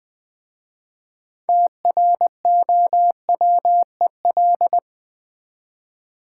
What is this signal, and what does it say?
Morse code, heard as a keyed tone, about 20 wpm, 700 Hz. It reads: TROWEL
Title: Easy Words with Hints - 20wpm